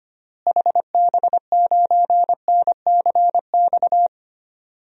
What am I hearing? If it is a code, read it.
Morse code, heard as a keyed tone, about 25 wpm, 700 Hz. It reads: HB9NCX
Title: DXCC-100 Call Signs - 6 Characters - Repeated 2x - 25wpm